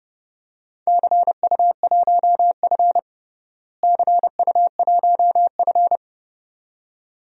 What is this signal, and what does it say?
Morse code, heard as a keyed tone, about 32 wpm, 700 Hz. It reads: CU1F CU1F
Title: DXCC-100 Call Signs - 4 Characters - Repeated 2x - 30wpm